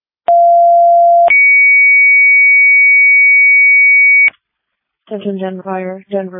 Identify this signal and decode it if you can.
Two-tone page, heard as a one-second tone, then a three-second tone, about 690 then 2250 Hz, with speech. It is heard from 0.3 to 4.3 s.
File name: Denver Fire 2025 04 26 13 17 55